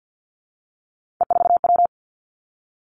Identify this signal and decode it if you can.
Morse code, heard as a keyed tone, about 50 wpm, 700 Hz. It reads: E4W
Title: Three Challenging Letter-Number Combinations 50wpm